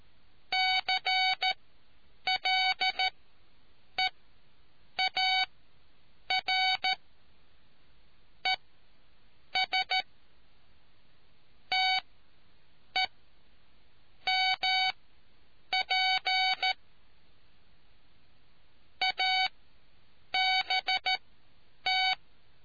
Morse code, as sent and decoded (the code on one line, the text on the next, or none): -.-. .-.. .- .-. / . ... / - . / -- .--. / .- -... -
CLEAR ES TE MP ABT